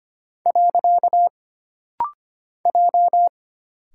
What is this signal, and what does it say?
Morse code, heard as a keyed tone, about 25 wpm, 700 Hz. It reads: . J
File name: Single Character (Letter:Number:Punctuation:Prosign) - Rapid-Fire - 25wpm